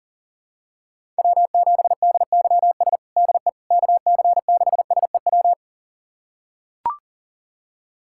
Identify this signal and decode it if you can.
Morse code, heard as a keyed tone, about 40 wpm, 700 Hz. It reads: W7DYS DE KC6SEW
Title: USA Callsign DE Callsign 40wpm